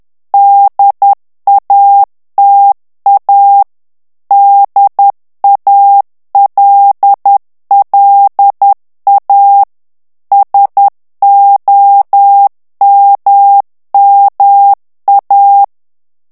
Morse code, heard as a keyed tone, about 10 wpm, 800 Hz. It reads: DATA DALLA SOMMA
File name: ITALIEN texte clair 2 vitesse 10 mots minute